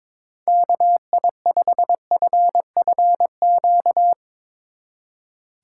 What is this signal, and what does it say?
Morse code, heard as a keyed tone, about 24 wpm, 700 Hz. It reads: KI5FFQ
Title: Speed Racing - USA 2x3 Call Signs 22wpm